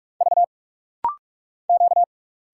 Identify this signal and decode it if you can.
Morse code, heard as a keyed tone, about 46 wpm, 690 Hz. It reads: V Q
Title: Single Letters - Rapid-Fire 45wpm